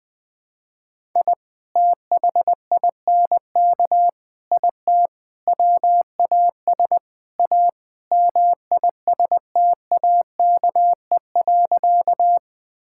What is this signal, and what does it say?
Morse code, heard as a keyed tone, about 21 wpm, 700 Hz. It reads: I THINK IT WAS A MISTAKE.
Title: Sentences with 5-Word Phrases - 20wpm